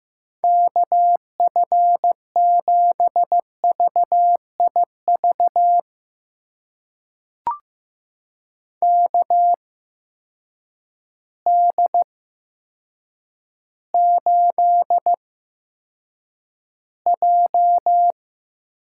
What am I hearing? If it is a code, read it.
Morse code, heard as a keyed tone, about 16 wpm, spaced as for 8 wpm, 700 Hz. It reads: KF7VIV KD8J
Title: USA Call Signs - 6 Characters - 8x Character Spacing - 15wpm